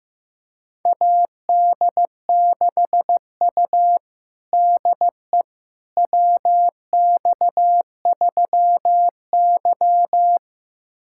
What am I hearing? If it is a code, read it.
Morse code, heard as a keyed tone, about 15 wpm, 700 Hz. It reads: AD6U DE WX3Y
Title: USA Callsign DE Callsign - 4 Characters - 15wpm